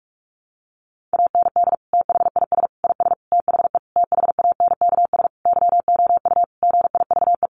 Morse code, heard as a keyed tone, about 44 wpm, 700 Hz. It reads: AND THIS IS THE THANKS YOU GIVE
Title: Sentences from Top 1200 Words 45wpm